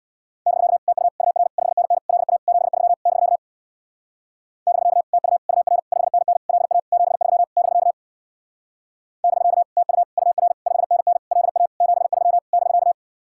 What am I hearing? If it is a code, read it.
Morse code, heard as a keyed tone, about 61 wpm, 700 Hz. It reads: <BK> TU UR 5NN LA 73 <BK> <BK> TU UR 5NN LA 73 <BK> <BK> TU UR 5NN LA 73 <BK>
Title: Speed-Racing Common QSO Phrases - 50wpm